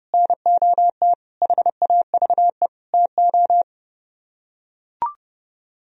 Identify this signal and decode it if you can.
Morse code, heard as a keyed tone, about 30 wpm, 700 Hz. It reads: NOT HAVE TO